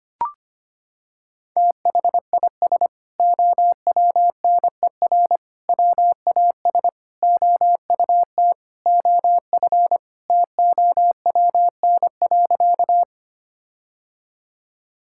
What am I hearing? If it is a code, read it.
Morse code, heard as a keyed tone, about 25 wpm, 700 Hz. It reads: THIS OWNER WAS OUT OF TOWN.
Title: Sentences from Top 1600 Words 25wpm